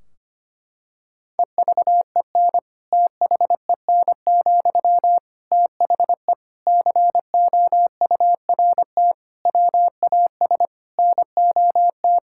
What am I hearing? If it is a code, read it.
Morse code, heard as a keyed tone, about 25 wpm, 700 Hz. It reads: EVEN THEN, THE COURT WAS NOT
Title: Sentences from Top 1600 Words 25wpm